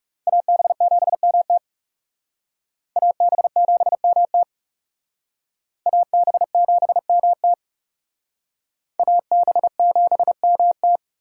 Morse code, heard as a keyed tone, about 37 wpm, 700 Hz. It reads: AB7MT AB7MT AB7MT AB7MT
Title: Speed Racing - USA Call Signs 30wpm